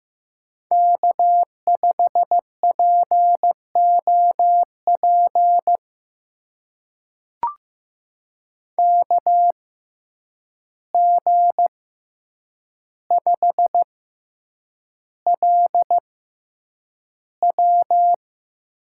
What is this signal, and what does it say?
Morse code, heard as a keyed tone, about 15 wpm, spaced as for 8 wpm, 700 Hz. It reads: K5POP KG5LW
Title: USA Call Signs - 6x Character Spacing - 15wpm